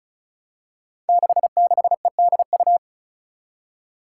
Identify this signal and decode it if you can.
Morse code, heard as a keyed tone, about 36 wpm, 700 Hz. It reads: 66EDU
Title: Five Challenging Letter-Number Combinations 35wpm